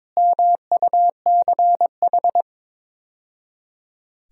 Morse code, heard as a keyed tone, about 22 wpm, 700 Hz. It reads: MUCH
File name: practice / 22wpm / Build A Phrase - 5-Word Phrases - 22wpm